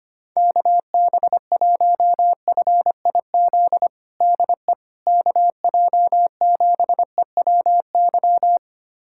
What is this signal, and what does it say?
Morse code, heard as a keyed tone, about 25 wpm, 700 Hz. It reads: KB1FIZ DE KJ7EWY
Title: USA Callsign DE Callsign 25wpm